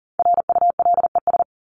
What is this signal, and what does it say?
Morse code, heard as a keyed tone, about 38 wpm, 700 Hz. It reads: RULES
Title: ES:S Suffix - Top 100 Words with Base Words 40wpm